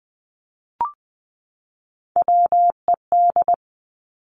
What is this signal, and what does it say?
Morse code, heard as a keyed tone, about 20 wpm, 700 Hz. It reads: WED
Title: Three Challenging Letter-Number Combinations 20wpm